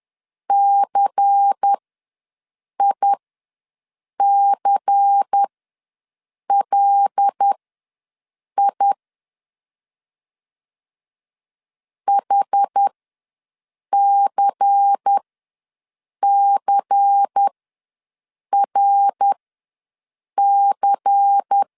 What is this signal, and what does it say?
Morse code, heard as a keyed tone, about 10 wpm, spaced as for 6 wpm, 800 Hz. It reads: CICLI HCCRC
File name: LEZIONI / 5 LBVC / 5 LBVC ws 10wpm 3733